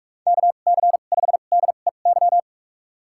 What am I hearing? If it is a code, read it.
Morse code, heard as a keyed tone, about 47 wpm, spaced as for 33 wpm, 700 Hz. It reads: KC5DEY
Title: USA Call Signs - 6 Characters - 2x Character Spacing - 45wpm